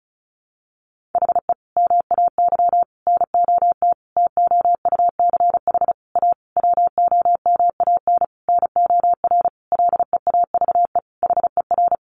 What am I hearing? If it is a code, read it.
Morse code, heard as a keyed tone, about 34 wpm, 700 Hz. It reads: HE MAY NOT TOUCH A WOMAN NOR LEAVE HER
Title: Sentences from Top 1500 Words 35wpm